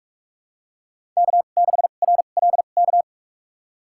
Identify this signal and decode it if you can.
Morse code, heard as a keyed tone, about 41 wpm, spaced as for 31 wpm, 700 Hz. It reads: K6RLK
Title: USA Call Signs - 5 Characters - 2x Character Spacing - 45wpm